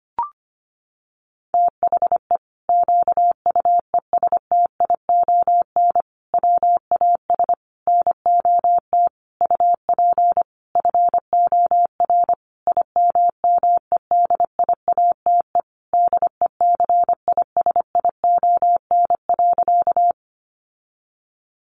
Morse code, heard as a keyed tone, about 25 wpm, 700 Hz. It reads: THE QUESTION WAS NOT UP FOR IMMEDIATE DECISION.
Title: Sentences from Top 1200 Words 25wpm